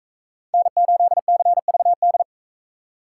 Morse code, heard as a keyed tone, about 38 wpm, 700 Hz. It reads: N8CVD